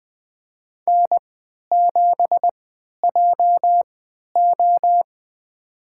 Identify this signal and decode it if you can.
Morse code, heard as a keyed tone, about 20 wpm, spaced as for 11 wpm, 700 Hz. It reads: N7JO
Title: USA Call Signs - 4 Characters - 3x Character Spacing - 20wpm